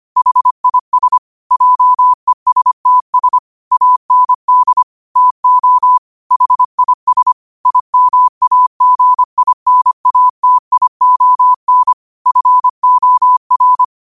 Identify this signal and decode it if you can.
Morse code, heard as a keyed tone, about 25 wpm, 1000 Hz. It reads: HIS JESTS AND TO HIS IMAGINATION FOR
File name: 25wpm